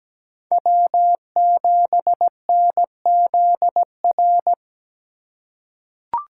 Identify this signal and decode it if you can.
Morse code, heard as a keyed tone, about 17 wpm, 700 Hz. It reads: W7NZR
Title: USA Call Signs - 5 Characters - 3x Character Spacing - 17wpm